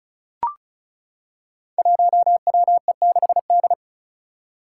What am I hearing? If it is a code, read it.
Morse code, heard as a keyed tone, about 36 wpm, 700 Hz. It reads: 1WE6D
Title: Five Challenging Letter-Number Combinations 35wpm